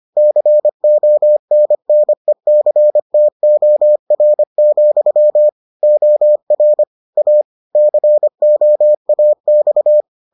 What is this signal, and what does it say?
Morse code, heard as a keyed tone, about 25 wpm, 600 Hz. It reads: CONNECTOR, OR A COAX